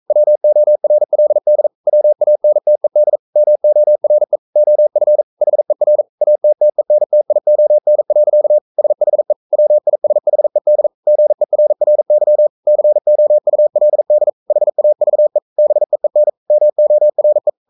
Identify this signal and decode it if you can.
Morse code, heard as a keyed tone, about 42 wpm, 600 Hz. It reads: WORLD WANTED MORE OF HER ATTENTION. SHE WISHED GERRY COULD HAVE BEEN MORE
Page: Romanen Never af Ken Follet på engelsk.